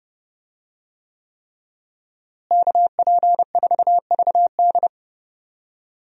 Morse code, heard as a keyed tone, about 32 wpm, 700 Hz. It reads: KP4VD
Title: DXCC-100 Call Signs - 5 Characters - Repeated 3x - 30wpm